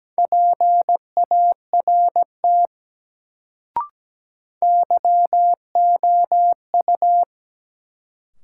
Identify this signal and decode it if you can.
Morse code, heard as a keyed tone, about 17 wpm, 700 Hz. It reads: PART YOU
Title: Sets of 2 Words - New Word Drill - 17wpm